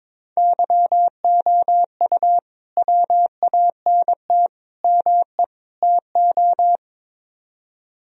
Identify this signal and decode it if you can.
Morse code, heard as a keyed tone, about 23 wpm, 700 Hz. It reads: YOU WANT ME TO
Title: Speed-Racing - Sets of 4 Words 22wpm